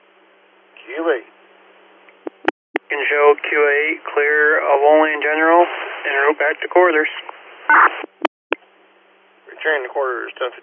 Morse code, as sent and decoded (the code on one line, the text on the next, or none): none